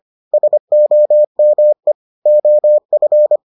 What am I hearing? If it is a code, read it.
Morse code, heard as a keyed tone, about 25 wpm, 600 Hz. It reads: SOME OF